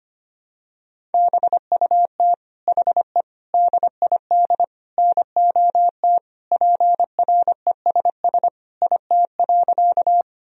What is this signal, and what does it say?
Morse code, heard as a keyed tone, about 25 wpm, 700 Hz. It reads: BUT HE DID NOT PRESS IT.
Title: Sentences from Top 1000 Words - Review Part 1 - 25wpm